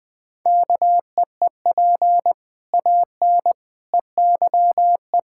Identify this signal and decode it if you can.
Morse code, heard as a keyed tone, about 21 wpm, 700 Hz. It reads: KEEP AN EYE
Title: Sets of 6 Words - 20wpm